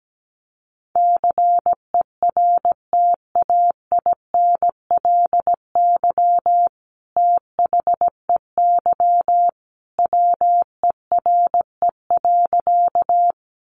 Morse code, heard as a keyed tone, about 17 wpm, 700 Hz. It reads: CERTAINLY THEY WERE.
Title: Sentences from Top 900 Words 17wpm